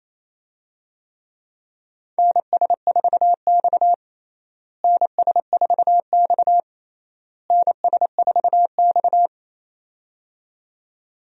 Morse code, heard as a keyed tone, about 28 wpm, 700 Hz. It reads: NS4X NS4X NS4X
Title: USA Call Signs - 4 Characters - Repeated 3x - 28wpm